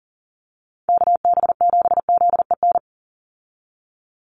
Morse code, heard as a keyed tone, about 40 wpm, 700 Hz. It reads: KB7ZEN
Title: USA Call Signs 40wpm